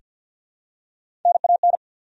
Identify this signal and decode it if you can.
Morse code, heard as a keyed tone, about 48 wpm, 700 Hz. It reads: NAN
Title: USA Common Names - Max 3 Letters - 50wpm